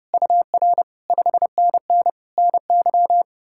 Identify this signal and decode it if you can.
Morse code, heard as a keyed tone, about 37 wpm, 700 Hz. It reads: UR 5NN NY
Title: Speed-Racing Common QSO Phrases - 30wpm